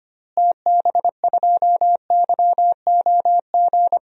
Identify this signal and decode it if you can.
Morse code, heard as a keyed tone, about 25 wpm, 700 Hz. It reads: TB2YOG
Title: DXCC-100 Call Signs - 6 Characters - Repeated 2x - 25wpm